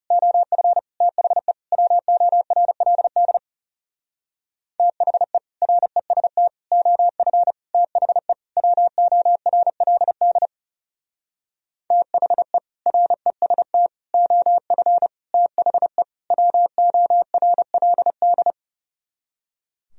Morse code, heard as a keyed tone, about 38 wpm, 700 Hz. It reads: OF THE WORLD THE REST OF THE WORLD THE REST OF THE WORLD